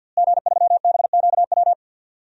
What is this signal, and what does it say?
Morse code, heard as a keyed tone, about 52 wpm, 700 Hz. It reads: G3BQW
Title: DXCC-100 Call Signs - 5 Characters - Repeated 2x - 50wpm